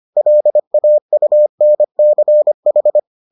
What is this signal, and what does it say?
Morse code, heard as a keyed tone, about 25 wpm, 600 Hz. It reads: LAUNCH